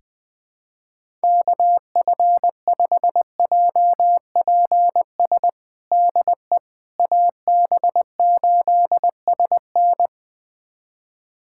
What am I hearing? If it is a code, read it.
Morse code, heard as a keyed tone, about 20 wpm, 700 Hz. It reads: KF5JPS DE AB8SN
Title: USA Callsign DE Callsign 20wpm